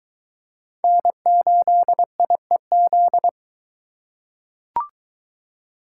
Morse code, heard as a keyed tone, about 21 wpm, 700 Hz. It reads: N8IEZ